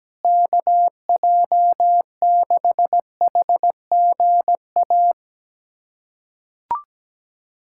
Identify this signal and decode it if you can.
Morse code, heard as a keyed tone, about 17 wpm, 700 Hz. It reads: KJ6HGA
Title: USA Callsign DE Callsign - 6 Characters - 17wpm